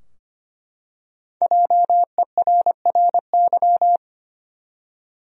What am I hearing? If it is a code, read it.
Morse code, heard as a keyed tone, about 25 wpm, 700 Hz. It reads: JERRY
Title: USA Common Names - Max 5 Letters - 25wpm